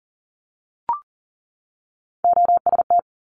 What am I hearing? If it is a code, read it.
Morse code, heard as a keyed tone, about 41 wpm, 700 Hz. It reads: OST
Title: Multi-letter Phonemes 40wpm